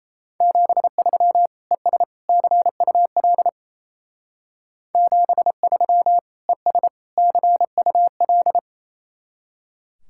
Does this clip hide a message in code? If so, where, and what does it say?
Yes: Morse code from 0.4 to 8.6 s, heard as a keyed tone, about 36 wpm, 700 Hz. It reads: 73 ES CUL 73 ES CUL